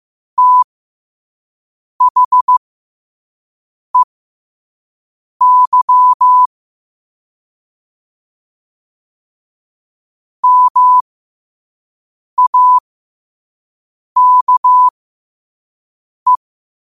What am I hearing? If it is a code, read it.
Morse code, heard as a keyed tone, about 15 wpm, spaced as for 5 wpm, 1000 Hz. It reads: THEY MAKE
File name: qotdfiles / 5wpm